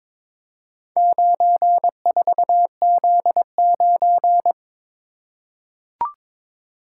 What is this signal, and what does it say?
Morse code, heard as a keyed tone, about 22 wpm, 700 Hz. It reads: 94Z9
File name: practice / 22wpm / Four Letter-Number Combinations - 22wpm